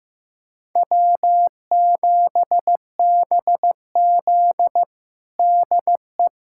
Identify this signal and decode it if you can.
Morse code, heard as a keyed tone, about 15 wpm, 700 Hz. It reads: W7BZ DE
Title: USA Callsign DE Callsign - 4 Characters - 15wpm